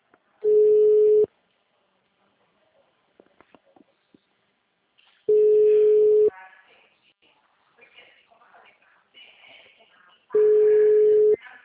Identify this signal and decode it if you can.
Ringback tone (Europe, 425 Hz), with speech.
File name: out- -2007- - - .600